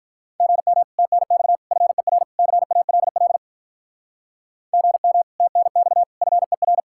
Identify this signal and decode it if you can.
Morse code, heard as a keyed tone, about 46 wpm, 700 Hz. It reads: GM TNX FER CALL GM TNX FER